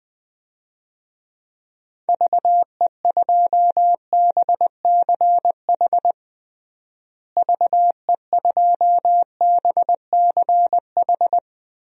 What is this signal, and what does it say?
Morse code, heard as a keyed tone, about 20 wpm, 700 Hz. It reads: VE2BCH VE2BCH
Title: DXCC-100 Call Signs - 6 Characters - Repeated 2x - 20wpm